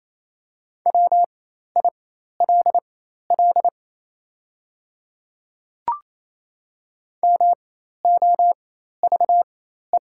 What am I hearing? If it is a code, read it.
Morse code, heard as a keyed tone, about 28 wpm, spaced as for 13 wpm, 700 Hz. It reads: WILL MOVE
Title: 4x Character Spacing - Top 300 Words - 28wpm